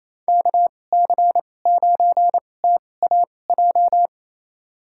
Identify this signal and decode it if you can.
Morse code, heard as a keyed tone, about 28 wpm, spaced as for 20 wpm, 700 Hz. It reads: KC9TAJ
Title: USA Call Signs - 6 Characters - 2x Character Spacing - 28wpm